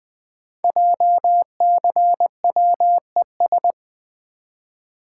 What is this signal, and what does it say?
Morse code, heard as a keyed tone, about 20 wpm, 700 Hz. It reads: JCWES
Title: Code Groups - 5 Letters - 20wpm